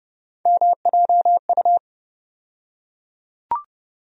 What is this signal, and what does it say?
Morse code, heard as a keyed tone, about 32 wpm, 700 Hz. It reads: MJU
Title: USA Call Sign Postfixes - 3 Letters 30wpm